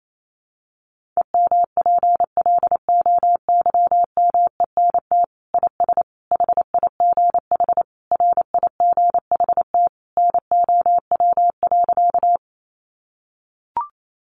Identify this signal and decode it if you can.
Morse code, heard as a keyed tone, about 28 wpm, 700 Hz. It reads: EMPLOYMENT IS HIGH RIGHT NOW.
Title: Sentences from Top 1500 Words 28wpm